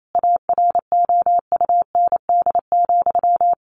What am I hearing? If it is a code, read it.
Morse code, heard as a keyed tone, about 28 wpm, 700 Hz. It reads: AROUND,
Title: Sentences from Top 1200 Words 28wpm